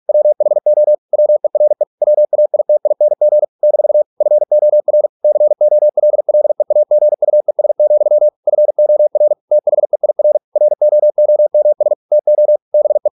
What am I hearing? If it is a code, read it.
Morse code, heard as a keyed tone, about 46 wpm, 600 Hz. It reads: WHO WERE WAITING - FOR COLLEAGUES, FOR THEIR ROOMS TO BE